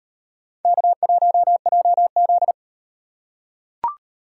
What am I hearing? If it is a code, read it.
Morse code, heard as a keyed tone, about 38 wpm, 700 Hz. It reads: K1JZ